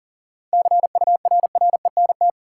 Morse code, heard as a keyed tone, about 39 wpm, 700 Hz. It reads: CURRENT
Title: Sentences from Top 1000 Words - Review Part 4 - New Word Drill - 40wpm